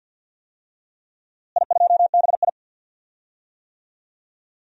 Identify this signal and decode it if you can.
Morse code, heard as a keyed tone, about 47 wpm, 700 Hz. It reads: I2BI
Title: DXCC-100 Call Signs - 4 Characters - Repeated 2x - 50wpm